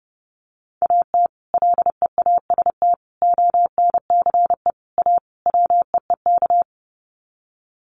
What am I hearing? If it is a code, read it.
Morse code, heard as a keyed tone, about 30 wpm, 700 Hz. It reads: AT LEAST ONCE A WEEK
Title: Sets of 5 Words Encore2 30wpm